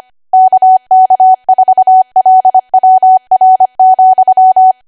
Morse code, heard as a keyed tone, about 25 wpm, 750 Hz. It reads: KK4LWR,
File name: Archive / 25 WPM / 25WPM